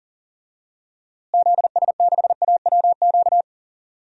Morse code, heard as a keyed tone, about 40 wpm, 700 Hz. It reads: ZS6AWQ
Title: DXCC-100 Call Signs - 6 Characters - Repeated 3x - 40wpm